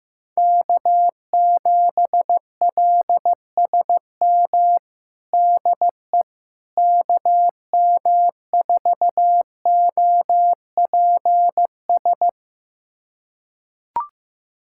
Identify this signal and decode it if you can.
Morse code, heard as a keyed tone, about 15 wpm, 700 Hz. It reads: K7LSM DE KM4OPS
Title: USA Callsign DE Callsign 15wpm